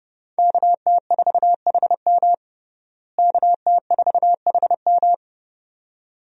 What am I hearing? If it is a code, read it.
Morse code, heard as a keyed tone, about 32 wpm, 700 Hz. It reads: KT4HM KT4HM
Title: USA Call Signs - 5 Characters - Repeated 2x - 30wpm